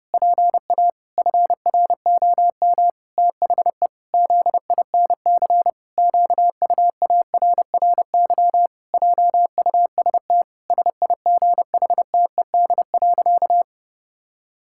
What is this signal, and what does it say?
Morse code, heard as a keyed tone, about 32 wpm, 700 Hz. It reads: PA FROM THE ZINC QUARRY JUST SIGHTED.
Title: Pangram Challenge v2 - 30wpm